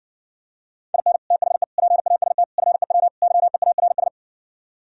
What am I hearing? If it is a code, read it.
Morse code, heard as a keyed tone, about 59 wpm, 700 Hz. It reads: IN THE PAST FEW YEARS